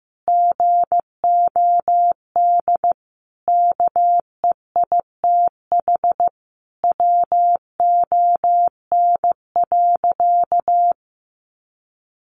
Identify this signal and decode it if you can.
Morse code, heard as a keyed tone, about 15 wpm, 700 Hz. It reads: GOD KEITH WON.
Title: A Sound Of Thunder 15wpm